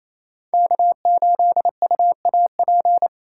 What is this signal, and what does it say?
Morse code, heard as a keyed tone, about 35 wpm, 700 Hz. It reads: K8UAP